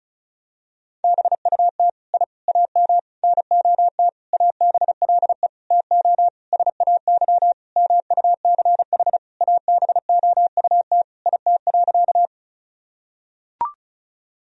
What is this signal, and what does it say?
Morse code, heard as a keyed tone, about 36 wpm, 700 Hz. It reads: BUT I AM NOT ABLE TO SAY MUCH ABOUT IT.
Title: Sentences from Top 1000 Words - Review Part 2 - 35wpm